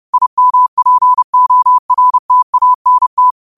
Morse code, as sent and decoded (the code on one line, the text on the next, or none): .. -- .--. --- .-. - .- -. -
IMPORTANT